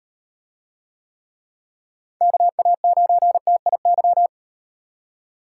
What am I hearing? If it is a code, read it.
Morse code, heard as a keyed tone, about 38 wpm, 700 Hz. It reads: KA9TIY